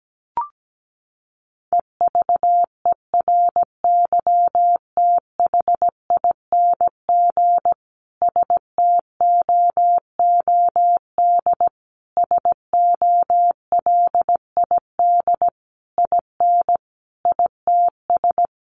Morse code, heard as a keyed tone, about 17 wpm, 700 Hz. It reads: EVERYTHING STOOD SOLID IN ITS
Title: Sentences from Top 1500 Words 17wpm